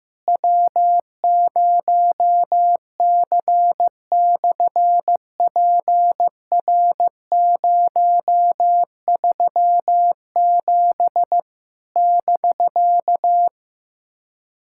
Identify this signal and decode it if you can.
Morse code, heard as a keyed tone, about 15 wpm, 700 Hz. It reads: W0C/PR037 <BK>
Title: SOTA QSO Practice 15wpm